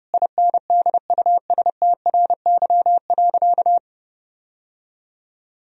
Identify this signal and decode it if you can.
Morse code, heard as a keyed tone, about 31 wpm, 700 Hz. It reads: INDUSTRY.
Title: Sentences from Top 1000 Words - Review Part 4 - 30wpm